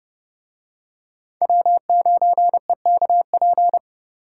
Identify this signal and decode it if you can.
Morse code, heard as a keyed tone, about 27 wpm, 700 Hz. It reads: W9EKP